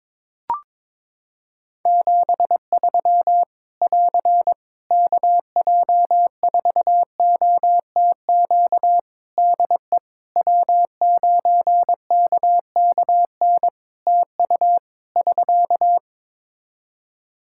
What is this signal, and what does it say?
Morse code, heard as a keyed tone, about 22 wpm, 700 Hz. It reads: 73 <AR> KJ4OTQ DE W9KKN TU <SK>
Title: Practice QSO - Exchange 4 - Ending - 22wpm